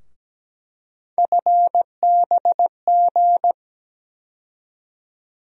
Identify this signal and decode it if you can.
Morse code, heard as a keyed tone, about 17 wpm, 700 Hz. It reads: FBG
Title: USA Call Sign Postfixes - 3 Letters 17wpm